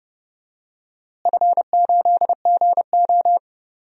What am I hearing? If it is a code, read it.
Morse code, heard as a keyed tone, about 30 wpm, 700 Hz. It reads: F8GO
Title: DXCC-100 Call Signs - 4 Characters - Repeated 3x - 30wpm